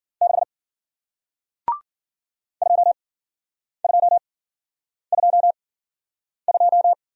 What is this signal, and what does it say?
Morse code, heard as a keyed tone, about 53 wpm, spaced as for 9 wpm, 700 Hz. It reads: 6 2222